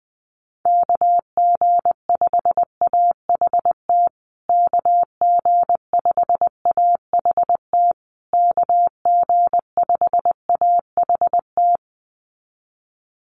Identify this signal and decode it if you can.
Morse code, heard as a keyed tone, about 20 wpm, 700 Hz. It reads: KG5AHT KG5AHT KG5AHT
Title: USA Call Signs Repeated 3x 20wpm